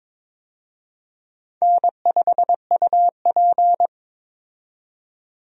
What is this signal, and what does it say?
Morse code, heard as a keyed tone, about 23 wpm, 700 Hz. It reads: N5UP
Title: USA Call Signs - 4 Characters - 22wpm